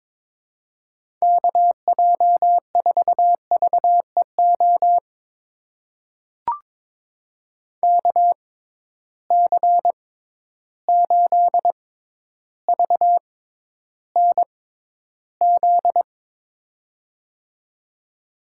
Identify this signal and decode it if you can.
Morse code, heard as a keyed tone, about 22 wpm, spaced as for 11 wpm, 700 Hz. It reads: KJ4VEO KC8VNZ
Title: USA Call Signs - 6 Characters - 6x Character Spacing - 22wpm